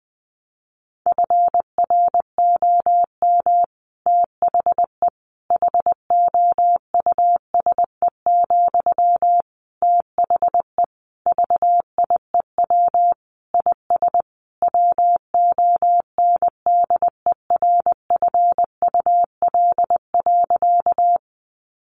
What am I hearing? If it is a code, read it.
Morse code, heard as a keyed tone, about 19 wpm, 700 Hz. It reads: FROM THE HOUSE, THE VIEW IS WONDERFUL.
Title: Sentences from Top 1100 Words 20wpm